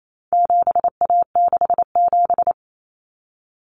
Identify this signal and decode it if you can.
Morse code, heard as a keyed tone, about 28 wpm, 700 Hz. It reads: 7A67
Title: Common Cut Numbers - 4-Digit 28wpm